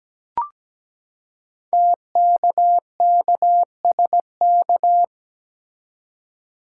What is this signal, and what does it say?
Morse code, heard as a keyed tone, about 17 wpm, 700 Hz. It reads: TKKSK
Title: Code Groups - 5 Letters - 17wpm